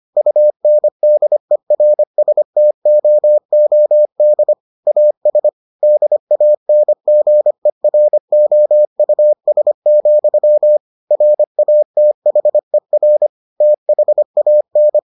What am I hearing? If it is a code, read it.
Morse code, heard as a keyed tone, about 25 wpm, 600 Hz. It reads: UNDERSTOOD AS DANGEROUS, RATHER THAN